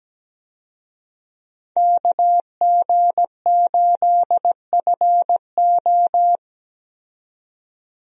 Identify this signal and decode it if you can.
Morse code, heard as a keyed tone, about 17 wpm, 700 Hz. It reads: KG8FO
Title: USA Call Signs - 5 Characters - 17wpm